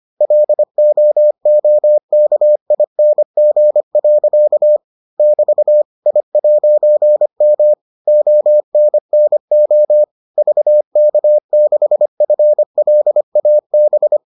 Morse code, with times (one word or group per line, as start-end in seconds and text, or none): 0.2-4.8 LOOKING.
5.2-5.8 <BT>
6.1-7.7 I'M
8.1-10.0 ONNO
10.4-14.2 VK6FLAB